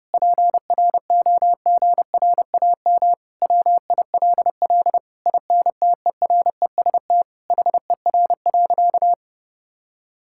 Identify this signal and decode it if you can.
Morse code, heard as a keyed tone, about 30 wpm, 700 Hz. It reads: PROGRAM WILL INTEREST HER.
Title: Sentences from Top 1000 Words - Review Part 5 - 30wpm